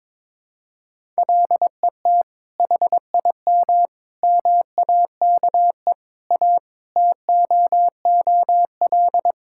Morse code, heard as a keyed tone, about 22 wpm, 700 Hz. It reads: LET HIM MAKE A TOOL
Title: Sentences from Top 1000 Words - Review Part 6 - 22wpm